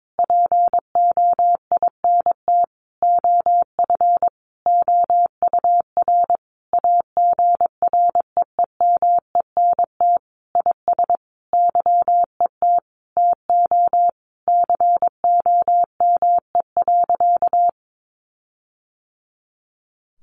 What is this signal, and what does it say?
Morse code, heard as a keyed tone, about 22 wpm, 700 Hz. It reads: POINT OF OUR AGREEMENT IS YET TO COME.
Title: Sentences from Top 1500 Words 22wpm